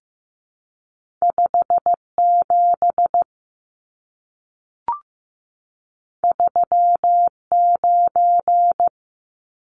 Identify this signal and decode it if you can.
Morse code, heard as a keyed tone, about 15 wpm, 700 Hz. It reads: 57 39
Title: Numbers - 2-Digit 15wpm